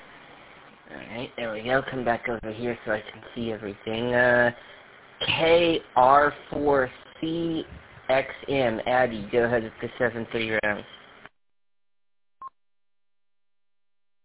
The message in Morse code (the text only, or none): none